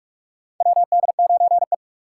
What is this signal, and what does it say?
Morse code, heard as a keyed tone, about 43 wpm, 700 Hz. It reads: WD9E
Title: USA Call Signs - 2x Character Spacing - 45wpm